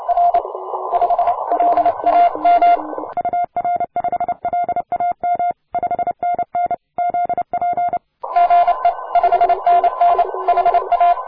Morse code, heard as a keyed tone, about 30 wpm, 700 Hz. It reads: R 5 LAM UR5LAM 5NN ZP GE 5NN HA